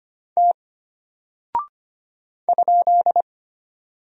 Morse code, heard as a keyed tone, about 25 wpm, 700 Hz. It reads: T ?
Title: Single Character (Letter:Number:Punctuation:Prosign) Alt 25wpm